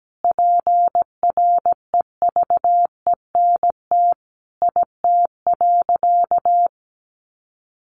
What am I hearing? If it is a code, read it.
Morse code, heard as a keyed tone, about 17 wpm, 700 Hz. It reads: PREVENT IT.
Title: Sentences from Top 1100 Words 17wpm